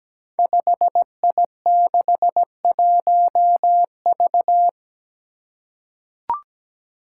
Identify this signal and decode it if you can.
Morse code, heard as a keyed tone, about 17 wpm, 700 Hz. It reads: 5I61V
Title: Five Challenging Letter-Number Combinations 17wpm